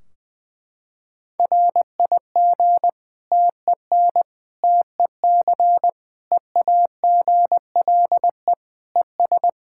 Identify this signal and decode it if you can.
Morse code, heard as a keyed tone, about 20 wpm, 700 Hz. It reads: RIG TEN TEC EAGLE ES
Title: Practice QSO - Exchange 2 - Info - 20wpm